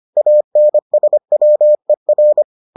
Morse code, heard as a keyed tone, about 25 wpm, 600 Hz. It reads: ANSWER